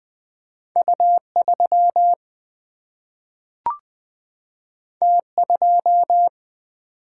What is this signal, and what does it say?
Morse code, heard as a keyed tone, about 20 wpm, 700 Hz. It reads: U 3 T 2
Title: Single-Letter Single-Number 20wpm